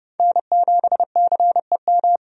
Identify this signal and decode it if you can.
Morse code, heard as a keyed tone, about 30 wpm, 700 Hz. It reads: N7CEM
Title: USA Call Signs - 5 Characters - 30wpm